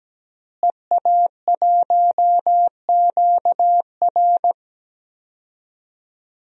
Morse code, heard as a keyed tone, about 17 wpm, 700 Hz. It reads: EA1QR
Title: DXCC-100 Call Signs - 5 Characters - Repeated 2x - 17wpm